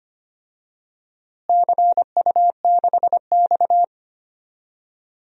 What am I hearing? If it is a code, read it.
Morse code, heard as a keyed tone, about 25 wpm, 700 Hz. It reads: CU6X
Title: DXCC-100 Call Signs - 4 Characters - Repeated 3x - 25wpm